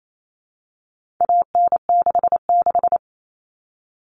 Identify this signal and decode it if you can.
Morse code, heard as a keyed tone, about 28 wpm, 700 Hz. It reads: AN66
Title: All Cut Numbers - 4-Digit 28wpm